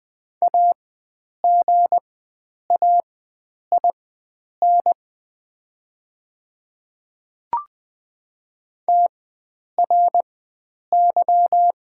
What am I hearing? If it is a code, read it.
Morse code, heard as a keyed tone, about 20 wpm, spaced as for 9 wpm, 700 Hz. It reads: AGAIN TRY
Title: 4x Character Spacing - Top 300 Words - 20wpm